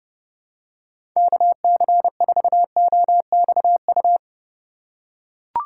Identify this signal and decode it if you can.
Morse code, heard as a keyed tone, about 30 wpm, 700 Hz. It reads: KC4OXU
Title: USA Call Signs - 6 Characters - 3x Character Spacing - 30wpm